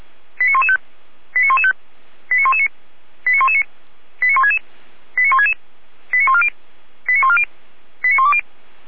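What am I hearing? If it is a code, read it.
Five-tone selective call (ZVEI), heard as quick runs of five tones, tones of about 70 ms. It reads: 89115 89115 89119 89119 89166 89166 89140 89140 89120
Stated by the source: ZVEI five-tone call received by SDR.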